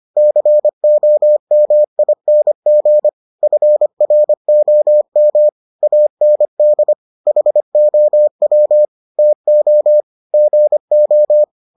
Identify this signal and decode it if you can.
Morse code, heard as a keyed tone, about 25 wpm, 600 Hz. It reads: COMING FROM AND HOW TO GO